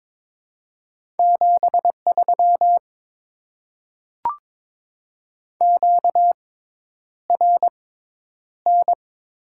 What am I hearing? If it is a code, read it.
Morse code, heard as a keyed tone, about 22 wpm, spaced as for 10 wpm, 700 Hz. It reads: 73 QRN
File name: Top 90 QSO Elements - 6x Character Spacing - 22wpm